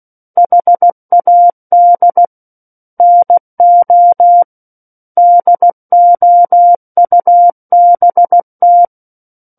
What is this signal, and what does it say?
Morse code, heard as a keyed tone, about 16 wpm, 700 Hz. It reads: HAD NO DOUBT
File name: Book One / 07-Chapter 7 - 16 WPM